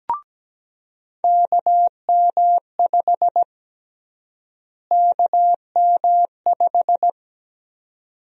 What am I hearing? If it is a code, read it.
Morse code, heard as a keyed tone, about 17 wpm, 700 Hz. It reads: KM5 KM5
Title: USA Call Sign Prefixes - Repeated 2x - 17wpm